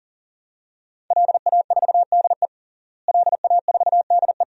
Morse code, heard as a keyed tone, about 40 wpm, 700 Hz. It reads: LA4DE LA4DE
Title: DXCC-100 Call Signs - 5 Characters - Repeated 2x - 40wpm